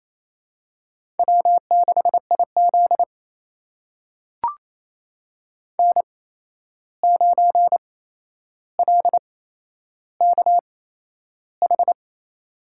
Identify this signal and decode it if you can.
Morse code, heard as a keyed tone, about 28 wpm, spaced as for 11 wpm, 700 Hz. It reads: W6IZ N9LKH
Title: USA Call Signs - 8x Character Spacing - 28wpm